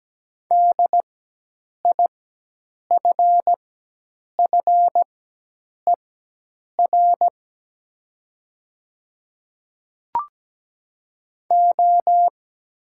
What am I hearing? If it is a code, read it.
Morse code, heard as a keyed tone, about 17 wpm, spaced as for 8 wpm, 700 Hz. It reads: DIFFER O
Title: 4x Character Spacing - Top 300 Words - 17wpm